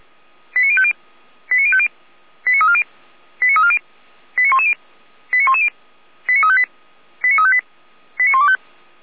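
Five-tone selective call (ZVEI), heard as quick runs of five tones, tones of about 70 ms. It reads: 89050 89050 89340 89340 89110 89110 89358 89358 89125
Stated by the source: ZVEI five-tone selective call received by SDR, from the Feuerwehr Orlamünde fire brigade audio server.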